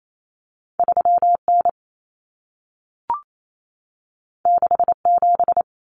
Morse code, heard as a keyed tone, about 28 wpm, 700 Hz. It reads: 3N 67
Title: Common Cut Numbers - 2-Digit 28wpm